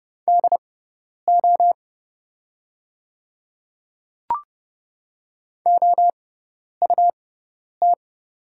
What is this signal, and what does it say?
Morse code, heard as a keyed tone, about 28 wpm, spaced as for 10 wpm, 700 Hz. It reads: DO OUT
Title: 6x Character Spacing - Top 100 Words - 30wpm